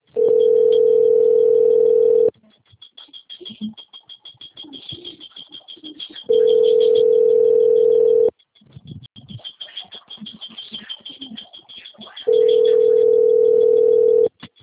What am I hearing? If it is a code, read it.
Ringback tone (North America), 440 + 480 Hz.